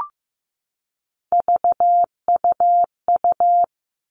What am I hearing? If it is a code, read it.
Morse code, heard as a keyed tone, about 15 wpm, 700 Hz. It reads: VUU
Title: All Cut Numbers - 3-Digit 15wpm